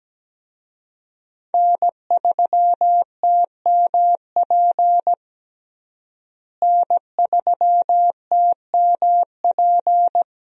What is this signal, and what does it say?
Morse code, heard as a keyed tone, about 17 wpm, 700 Hz. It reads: N3TMP N3TMP
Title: USA Call Signs - 5 Characters - Repeated 2x - 17wpm